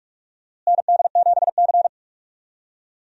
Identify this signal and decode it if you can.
Morse code, heard as a keyed tone, about 44 wpm, 700 Hz. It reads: ND7C
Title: USA Call Signs - 4 Characters - Repeated 3x - 45wpm